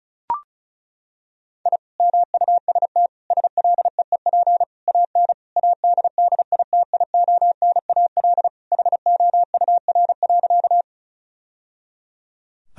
Morse code, heard as a keyed tone, about 36 wpm, 700 Hz. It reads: I MUST SLEEP AN ADDITIONAL HOUR.
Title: Sentences from Top 1000 Words - Review Part 4 - 35wpm